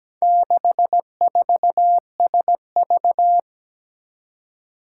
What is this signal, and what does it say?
Morse code, heard as a keyed tone, about 17 wpm, 700 Hz. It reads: 64SV
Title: Four Challenging Letter-Number Combinations 17wpm